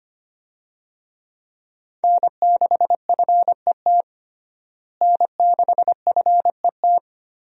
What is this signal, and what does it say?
Morse code, heard as a keyed tone, about 25 wpm, 700 Hz. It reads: N6FET N6FET
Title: USA Call Signs - 5 Characters - Repeated 2x - 25wpm